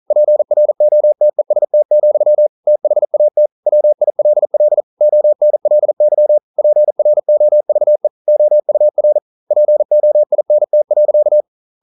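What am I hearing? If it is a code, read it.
Morse code, heard as a keyed tone, about 42 wpm, 600 Hz. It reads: PROTEST, THAT WILL ONLY PROVE OUR POINT.
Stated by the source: Morse code practice at 41 wpm, from CW med Gnister.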